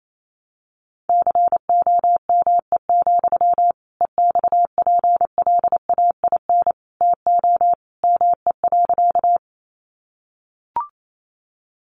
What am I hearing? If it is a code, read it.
Morse code, heard as a keyed tone, about 28 wpm, 700 Hz. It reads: COME, EXPLAIN TO ME.
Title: Sentences from Top 1100 Words 28wpm